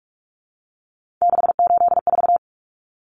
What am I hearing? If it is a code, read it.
Morse code, heard as a keyed tone, about 44 wpm, 700 Hz. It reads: 684
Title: Numbers - 3-Digit 45wpm